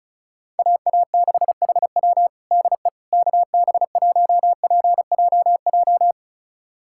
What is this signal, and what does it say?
Morse code, heard as a keyed tone, about 36 wpm, 700 Hz. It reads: AA6HW DE KB1PJJ
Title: USA Callsign DE Callsign 35wpm